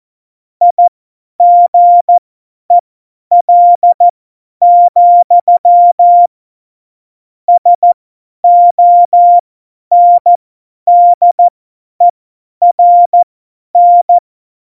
Morse code, heard as a keyed tone, about 14 wpm, spaced as for 10 wpm, 700 Hz. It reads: IGEL, SONDERN
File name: Morsekurs / Witze / Witz 28. November 2017, 50 F70